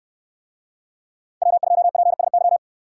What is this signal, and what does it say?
Morse code, heard as a keyed tone, about 64 wpm, 700 Hz. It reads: W2PSY